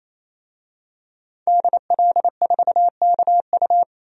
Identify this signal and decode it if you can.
Morse code, heard as a keyed tone, about 27 wpm, 700 Hz. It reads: DL4KU
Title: DXCC-100 Call Signs - 5 Characters - 28wpm